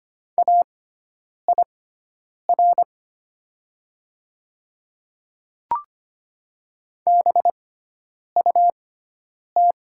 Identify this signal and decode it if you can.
Morse code, heard as a keyed tone, about 25 wpm, spaced as for 9 wpm, 700 Hz. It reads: AIR BUT
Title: 6x Character Spacing - Top 300 Words - 25wpm